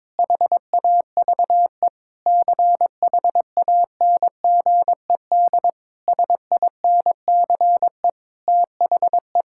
Morse code, heard as a keyed tone, about 22 wpm, 700 Hz. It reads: HAVE CHANGED SINCE THE
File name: Sentences with 7-Word Phrases - 22wpm